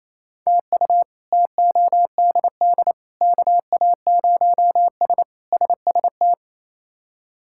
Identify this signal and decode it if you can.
Morse code, heard as a keyed tone, about 27 wpm, 700 Hz. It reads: TU TODD KA0S SST
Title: K1USN SST QSO Practice - 28wpm